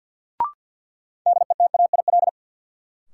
Morse code, heard as a keyed tone, about 52 wpm, 700 Hz. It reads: DETAIL
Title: Sentences from Top 1500 Words - New Word Drill - 50wpm